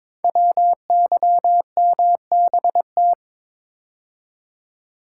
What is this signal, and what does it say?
Morse code, heard as a keyed tone, about 22 wpm, 700 Hz. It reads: WYMBT
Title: Code Groups - 5 Letters - 22wpm